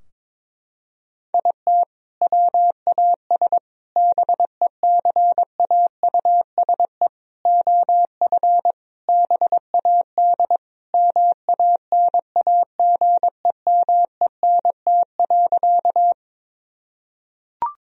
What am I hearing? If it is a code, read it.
Morse code, heard as a keyed tone, about 21 wpm, 700 Hz. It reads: IT WAS BECAUSE OF BAD MANAGEMENT.
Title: Sentences from Top 1000 Words - Review Part 4 - 22wpm